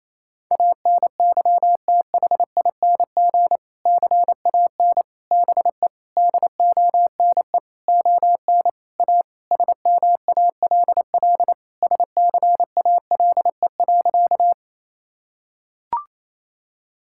Morse code, heard as a keyed tone, about 28 wpm, 700 Hz. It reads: ANYTHING CAN BE DONE ON A SMALL SCALE.
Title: Sentences from Top 1000 Words - Review Part 5 - 28wpm